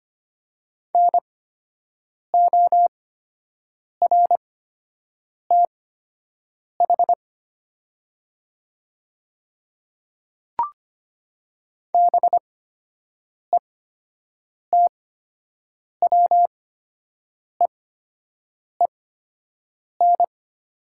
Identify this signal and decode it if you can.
Morse code, heard as a keyed tone, about 25 wpm, spaced as for 7 wpm, 700 Hz. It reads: NORTH BETWEEN
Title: 8x Character Spacing - Top 500 Words - 25wpm